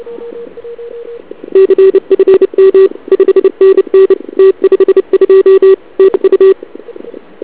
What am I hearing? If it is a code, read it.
Morse code, heard as a keyed tone, about 29 wpm, 380 Hz. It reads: CFM 5NN T52 TU
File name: ARCHIV / KX3 / CW pileup